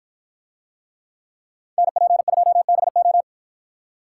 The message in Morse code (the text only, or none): NP2BO